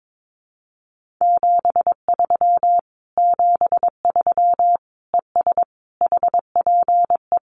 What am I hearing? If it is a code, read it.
Morse code, heard as a keyed tone, about 22 wpm, 700 Hz. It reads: 73 73 ES HPE
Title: Common QSO Phrases Encore 22wpm